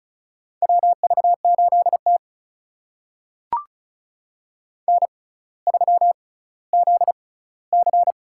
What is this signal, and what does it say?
Morse code, heard as a keyed tone, about 36 wpm, spaced as for 18 wpm, 700 Hz. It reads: WV8T N3ZC
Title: USA Call Signs - 4 Characters - 6x Character Spacing - 35wpm